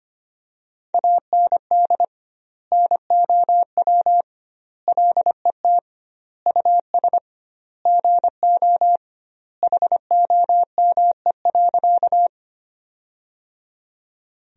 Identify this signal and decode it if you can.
Morse code, heard as a keyed tone, about 25 wpm, 700 Hz. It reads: AND NOW LET US GO HOME.
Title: 2x Word-Spacing - Sentences from Top 300 Words - 25wpm